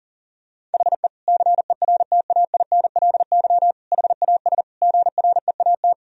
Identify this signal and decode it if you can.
Morse code, heard as a keyed tone, about 40 wpm, 700 Hz. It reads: HE CERTAINLY HAS GREAT
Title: Sentences from Top 1000 Words - Review Part 3 - 40wpm